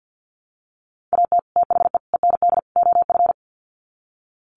Morse code, heard as a keyed tone, about 48 wpm, 700 Hz. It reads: AT THE END OF
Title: Sets of 4 Words 50wpm